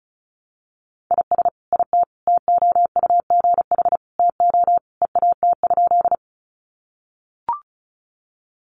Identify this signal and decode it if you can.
Morse code, heard as a keyed tone, about 34 wpm, 700 Hz. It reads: IS IT TOUGH TO EAT?
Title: Sentences from Top 1200 Words 35wpm